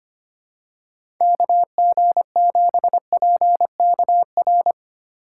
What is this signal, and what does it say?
Morse code, heard as a keyed tone, about 25 wpm, 700 Hz. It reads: KG7PKR
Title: USA Call Signs - 3x Character Spacing - 25wpm